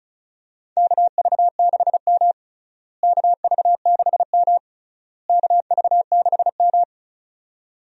Morse code, heard as a keyed tone, about 36 wpm, 700 Hz. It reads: KV6M KV6M KV6M
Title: USA Call Signs - 4 Characters - Repeated 3x - 35wpm